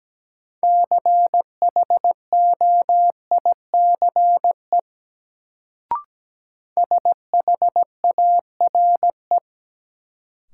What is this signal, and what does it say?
Morse code, heard as a keyed tone, about 17 wpm, 700 Hz. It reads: CHOICE SHARE
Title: Sentences from Top 900 Words - New Word Drill - 17wpm